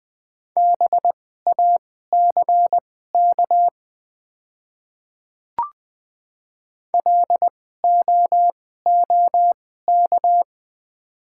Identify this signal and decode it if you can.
Morse code, heard as a keyed tone, about 20 wpm, spaced as for 14 wpm, 700 Hz. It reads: BACK LOOK
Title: 2x Character Spacing - Top 500 Words - 20wpm